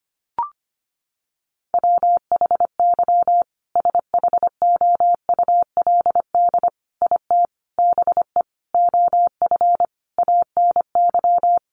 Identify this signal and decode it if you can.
Morse code, heard as a keyed tone, about 25 wpm, 700 Hz. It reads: WHY SHOULD IT BE OF ANY
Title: Sentences from Top 1200 Words 25wpm